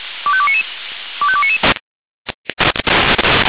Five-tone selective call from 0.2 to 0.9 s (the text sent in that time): 25299